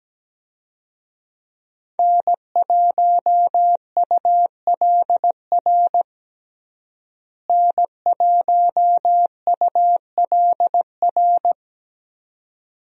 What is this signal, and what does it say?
Morse code, heard as a keyed tone, about 17 wpm, 700 Hz. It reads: N1ULR N1ULR
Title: USA Call Signs - 5 Characters - Repeated 2x - 17wpm